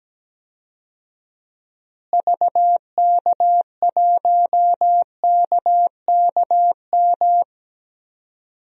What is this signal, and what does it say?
Morse code, heard as a keyed tone, about 17 wpm, 700 Hz. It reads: VK1KKM
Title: DXCC-100 Call Signs - 6 Characters - Repeated 3x - 17wpm